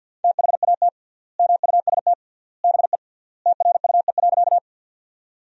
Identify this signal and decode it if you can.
Morse code, heard as a keyed tone, about 50 wpm, 700 Hz. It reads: THAT MUST BE TRUE.
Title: 3x Word-Spacing - Sentences from Top 400 Words - 50wpm